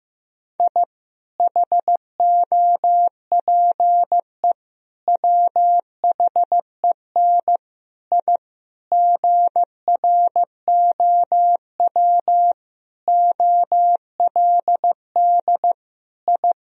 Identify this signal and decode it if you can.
Morse code, heard as a keyed tone, about 15 wpm, 700 Hz. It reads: I HOPE WHEN I GROW OLD I
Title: Sentences from Top 1000 Words - Review Part 1 - 15wpm